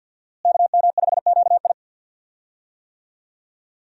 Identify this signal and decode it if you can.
Morse code, heard as a keyed tone, about 53 wpm, 700 Hz. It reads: KM5QI
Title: USA Call Signs - 5 Characters - Repeated 3x - 50wpm